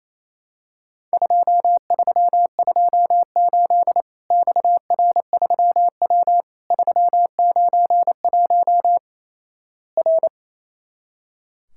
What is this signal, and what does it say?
Morse code, heard as a keyed tone, about 28 wpm, 700 Hz. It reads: 2328 XR3W 391 R
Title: Worked All Europe DX Contest - QTC Exchange - 28wpm